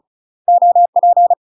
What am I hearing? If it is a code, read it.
Morse code, heard as a keyed tone, about 33 wpm, 700 Hz. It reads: OP
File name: UK MASTER QSOs TRAINING / 35WPM / UK MASTER QSOs 0007